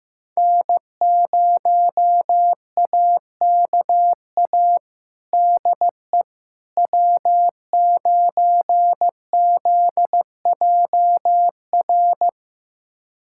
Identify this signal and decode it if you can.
Morse code, heard as a keyed tone, about 16 wpm, 700 Hz. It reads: N0AKA DE W9ZJR
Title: USA Callsign DE Callsign - 5 Characters - 15wpm